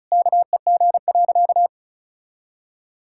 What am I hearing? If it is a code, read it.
Morse code, heard as a keyed tone, about 36 wpm, 700 Hz. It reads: KEG.
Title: Pangram Challenge v2 - 35wpm